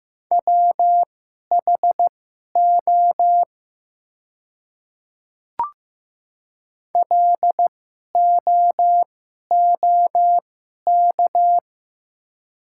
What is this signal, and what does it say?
Morse code, heard as a keyed tone, about 16 wpm, spaced as for 11 wpm, 700 Hz. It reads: WHO LOOK
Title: 2x Character Spacing - Top 300 Words - 15wpm